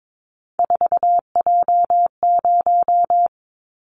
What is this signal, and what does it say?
Morse code, heard as a keyed tone, about 22 wpm, 700 Hz. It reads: 4J0
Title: Callsign Prefixes for Top 100 DXCC - 22wpm